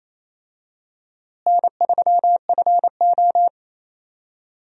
Morse code, heard as a keyed tone, about 28 wpm, 700 Hz. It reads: N3FO
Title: USA Call Signs - 4 Characters - Repeated 2x - 28wpm